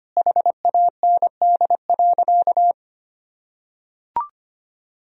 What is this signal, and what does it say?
Morse code, heard as a keyed tone, about 25 wpm, 700 Hz. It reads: HAND.
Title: Sentences from Top 1000 Words - Review Part 3 - 25wpm